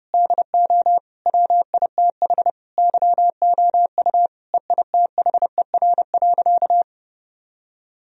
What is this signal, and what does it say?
Morse code, heard as a keyed tone, about 32 wpm, 700 Hz. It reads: DO WITH YOU EITHER.
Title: Sentences from Top 1000 Words - Review Part 4 - 30wpm